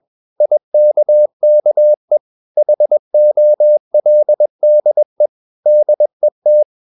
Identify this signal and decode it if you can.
Morse code, heard as a keyed tone, about 21 wpm, 600 Hz. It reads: IKKE HOLDE DET
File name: Ferie QRS 21wpm